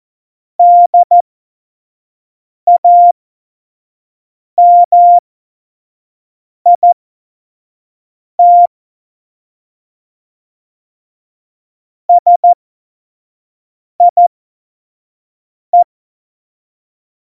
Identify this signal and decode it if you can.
Morse code, heard as a keyed tone, about 14 wpm, spaced as for 5 wpm, 700 Hz. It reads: DAMIT SIE
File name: Morsekurs / Witze / Witz 30. Oktober 2018, 25 F70